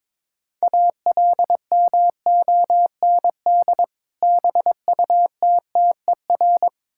Morse code, heard as a keyed tone, about 21 wpm, 700 Hz. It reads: ALMOND BUTTER
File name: Difficult Words with Hints - 22wpm